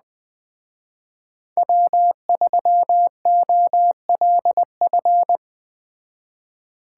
Morse code, heard as a keyed tone, about 20 wpm, 700 Hz. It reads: W3OLF